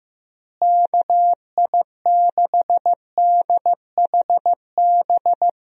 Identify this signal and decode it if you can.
Morse code, heard as a keyed tone, about 15 wpm, 700 Hz. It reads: KI6DHB
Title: USA Call Signs - 6 Characters - Repeated 3x - 15wpm